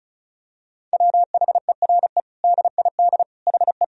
Morse code, heard as a keyed tone, about 34 wpm, 690 Hz. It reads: WHERE DID HE
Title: Sentences from Top 200 Words and Abbreviations 35wpm